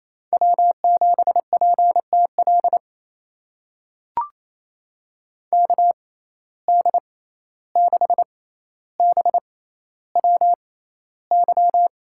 Morse code, heard as a keyed tone, about 28 wpm, spaced as for 14 wpm, 700 Hz. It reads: W7PTL KD6BWY
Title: USA Call Signs - 6x Character Spacing - 28wpm